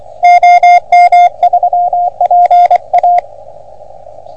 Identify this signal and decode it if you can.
Morse code, heard as a keyed tone, about 24 wpm, 710 Hz. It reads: OM3PA
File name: OM3PA PILUP CUC 388